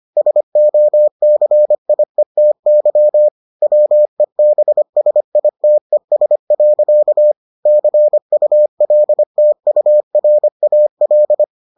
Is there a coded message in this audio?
Yes: Morse code from 0.2 to 11.4 s, heard as a keyed tone, about 25 wpm, 600 Hz. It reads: SOCIETY WEBSITES. CULTURAL